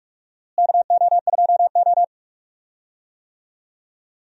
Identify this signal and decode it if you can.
Morse code, heard as a keyed tone, about 42 wpm, 700 Hz. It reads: KO2O
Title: USA Call Signs - 4 Characters - Repeated 2x - 45wpm